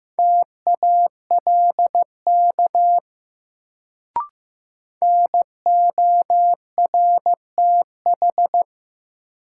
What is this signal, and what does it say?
Morse code, heard as a keyed tone, about 15 wpm, 700 Hz. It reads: TALK NORTH
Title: Sentences from Top 400 Words - New Word Drill - 15wpm